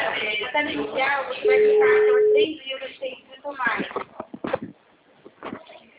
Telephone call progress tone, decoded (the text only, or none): ringback tone